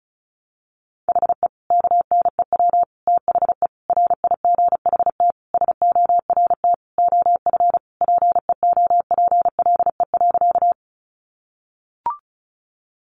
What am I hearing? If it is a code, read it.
Morse code, heard as a keyed tone, about 34 wpm, 700 Hz. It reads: HE KNEW THE RIGHT SORT OF PEOPLE.
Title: Sentences from Top 1300 Words 35wpm